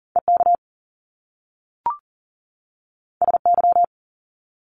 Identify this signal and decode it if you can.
Morse code, heard as a keyed tone, about 40 wpm, 700 Hz. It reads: EK SY
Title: Top Two Letter Combinations in Words 40wpm